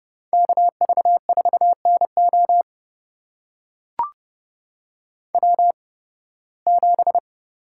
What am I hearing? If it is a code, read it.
Morse code, heard as a keyed tone, about 30 wpm, spaced as for 19 wpm, 700 Hz. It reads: KV4NO W7
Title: USA Call Signs - 5 Characters - 8x Character Spacing - 30wpm